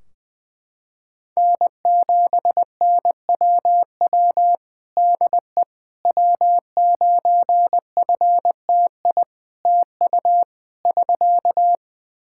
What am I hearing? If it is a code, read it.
Morse code, heard as a keyed tone, about 20 wpm, 700 Hz. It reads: N7NWW DE W9FTI TU <SK>
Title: Practice QSO - Exchange 4 - Ending - 20wpm